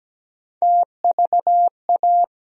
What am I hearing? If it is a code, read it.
Morse code, heard as a keyed tone, about 17 wpm, 700 Hz. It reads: TVA
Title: USA Call Sign Postfixes - 3 Letters 17wpm